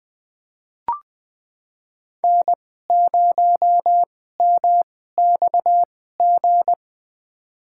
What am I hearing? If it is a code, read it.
Morse code, heard as a keyed tone, about 20 wpm, spaced as for 14 wpm, 700 Hz. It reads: N0MXG
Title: USA Call Signs - 5 Characters - 2x Character Spacing - 20wpm